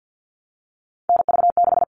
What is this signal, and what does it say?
Morse code, heard as a keyed tone, about 50 wpm, 700 Hz. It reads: NV6
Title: All Cut Numbers - 3-Digit 50wpm